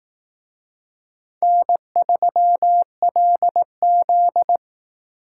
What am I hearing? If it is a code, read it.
Morse code, heard as a keyed tone, about 18 wpm, 700 Hz. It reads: N3LZ
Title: Speed Racing - USA Call Signs 15wpm